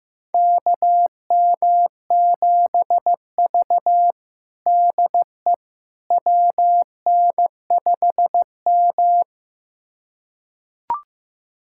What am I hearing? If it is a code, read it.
Morse code, heard as a keyed tone, about 16 wpm, 700 Hz. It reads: KM7V DE WN5M
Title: USA Callsign DE Callsign - 4 Characters - 15wpm